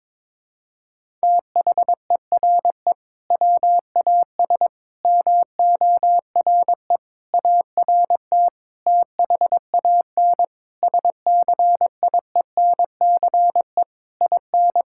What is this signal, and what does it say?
Morse code, heard as a keyed tone, about 22 wpm, 700 Hz. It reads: THERE WAS MORE ART THAN SCIENCE IN
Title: Sentences from Top 1000 Words - Review Part 6 - 22wpm